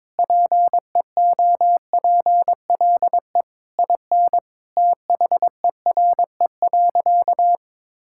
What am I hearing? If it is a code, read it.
Morse code, heard as a keyed tone, about 22 wpm, 700 Hz. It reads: PEOPLE IN THERE.
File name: practice / 22wpm / Sentences with 3-Word Phrases - 22wpm